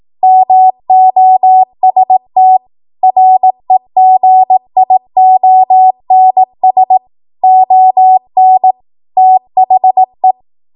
Morse code, heard as a keyed tone, about 18 wpm, 750 Hz. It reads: MOST REGIONS ON THE
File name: Morse / 18WPM